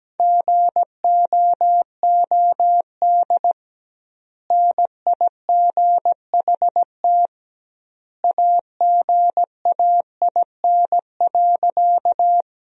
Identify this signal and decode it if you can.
Morse code, heard as a keyed tone, about 17 wpm, 700 Hz. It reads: GOOD NIGHT AGAIN.
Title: 2x Word-Spacing - Sentences from Top 300 Words - 17wpm